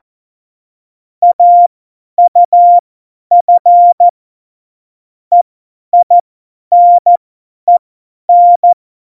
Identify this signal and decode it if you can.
Morse code, heard as a keyed tone, about 14 wpm, spaced as for 10 wpm, 700 Hz. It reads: AUF EINEN
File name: Morsekurs / Witze / Witz 15. Dezember 2020, 50 F70